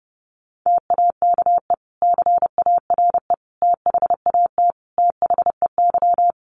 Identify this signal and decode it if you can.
Morse code, heard as a keyed tone, about 30 wpm, 700 Hz. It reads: TAKE CARE THAT THEY
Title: Sentences from Top 1500 Words 30wpm